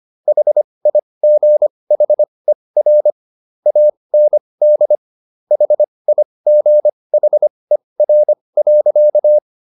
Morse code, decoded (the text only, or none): HIGHER AND HIGHER.